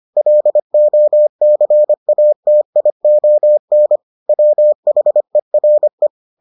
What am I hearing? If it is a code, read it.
Morse code, heard as a keyed tone, about 25 wpm, 600 Hz. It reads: LOCATION WHERE